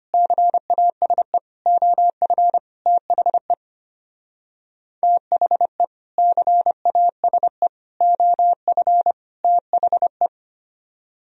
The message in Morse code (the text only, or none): CASE OF THE THE CASE OF THE